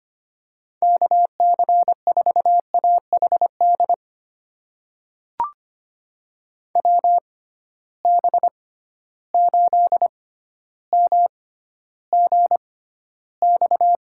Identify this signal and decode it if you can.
Morse code, heard as a keyed tone, about 25 wpm, spaced as for 13 wpm, 700 Hz. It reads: KC4AHD WB8MGX
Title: USA Call Signs - 6 Characters - 6x Character Spacing - 25wpm